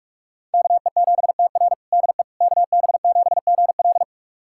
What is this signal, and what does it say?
Morse code, heard as a keyed tone, about 46 wpm, 700 Hz. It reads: KE7TR DE KB7GL
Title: USA Callsign DE Callsign - 5 Characters - 45wpm